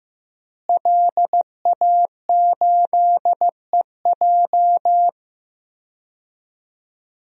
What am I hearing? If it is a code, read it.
Morse code, heard as a keyed tone, about 15 wpm, 700 Hz. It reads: LA8EJ
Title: DXCC-100 Call Signs - 5 Characters - Repeated 2x - 15wpm